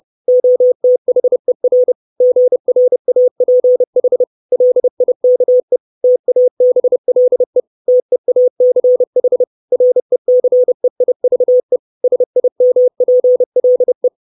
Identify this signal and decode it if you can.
Morse code, heard as a keyed tone, about 30 wpm, 500 Hz. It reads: OTHER GRAPH LIKE TABLE TEACH RECEIVE SIMPLE
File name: 632 Words 30WPM